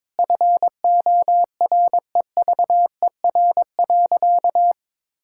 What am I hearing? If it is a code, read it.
Morse code, heard as a keyed tone, about 22 wpm, 700 Hz. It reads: FOREVER.
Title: Sentences from Top 1600 Words 22wpm